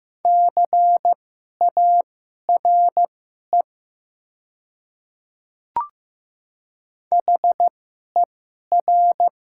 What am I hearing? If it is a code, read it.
Morse code, heard as a keyed tone, about 15 wpm, spaced as for 11 wpm, 700 Hz. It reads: CARE HER
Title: 2x Character Spacing - Top 500 Words - 15wpm